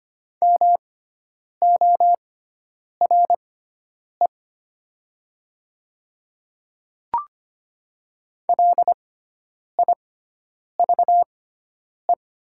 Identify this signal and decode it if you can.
Morse code, heard as a keyed tone, about 25 wpm, spaced as for 9 wpm, 700 Hz. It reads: MORE LIVE
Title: 6x Character Spacing - Top 300 Words - 25wpm